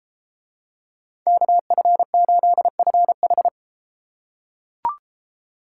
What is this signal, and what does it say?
Morse code, heard as a keyed tone, about 33 wpm, 700 Hz. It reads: KF8FH